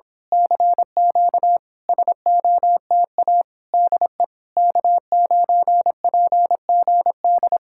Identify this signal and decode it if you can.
Morse code, heard as a keyed tone, about 26 wpm, 700 Hz. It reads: CQ SOTA DE K9PGD
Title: Speed-Racing Common QSO Phrases - 22wpm